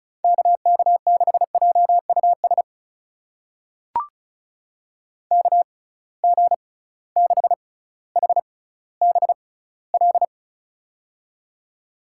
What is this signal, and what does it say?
Morse code, heard as a keyed tone, about 36 wpm, spaced as for 18 wpm, 700 Hz. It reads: KK6JUS KG6HBL
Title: USA Call Signs - 6x Character Spacing - 35wpm